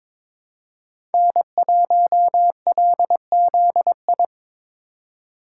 Morse code, heard as a keyed tone, about 22 wpm, 700 Hz. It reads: N1LZI
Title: USA Call Signs - 5 Characters - 3x Character Spacing - 22wpm